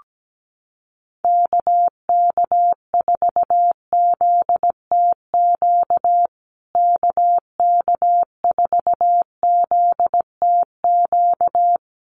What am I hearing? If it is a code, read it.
Morse code, heard as a keyed tone, about 17 wpm, 700 Hz. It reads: KK4ZTQ KK4ZTQ
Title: USA Call Signs Repeated 3x 17wpm